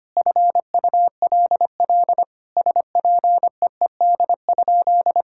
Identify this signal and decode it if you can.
Morse code, heard as a keyed tone, about 25 wpm, 700 Hz. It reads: FULL SPEED?
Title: Sentences from Top 1000 Words - Review Part 5 - 25wpm